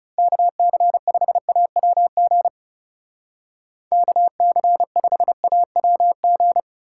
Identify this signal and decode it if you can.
Morse code, heard as a keyed tone, about 38 wpm, 700 Hz. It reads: KC5AWG KC5AWG